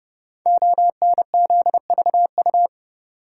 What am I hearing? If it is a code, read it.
Morse code, heard as a keyed tone, about 32 wpm, 700 Hz. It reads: ONZVU
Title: Code Groups - 5 Letter:Numbers - 30wpm